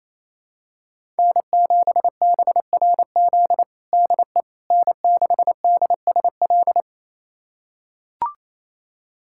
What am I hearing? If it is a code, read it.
Morse code, heard as a keyed tone, about 28 wpm, 700 Hz. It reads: N7BRZ DE N6DSL
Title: USA Callsign DE Callsign - 5 Characters - 28wpm